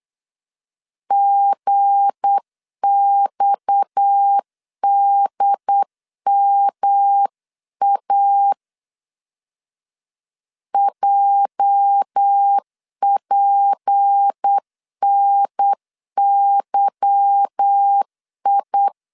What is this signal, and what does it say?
Morse code, heard as a keyed tone, about 8 wpm, 800 Hz. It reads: GXDMA JPNYI
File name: LEZIONI / 8 YJ45 / 8 YJ 8wpm 3313